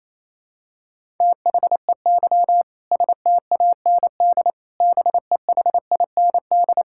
Morse code, heard as a keyed tone, about 28 wpm, 700 Hz. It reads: THEY STAND BEHIND
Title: Sentences from Top 1000 Words - Review Part 5 - 28wpm